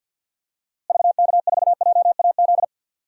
Morse code, heard as a keyed tone, about 52 wpm, 700 Hz. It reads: VK4JAZ
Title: DXCC-100 Call Signs - 6 Characters - Repeated 3x - 50wpm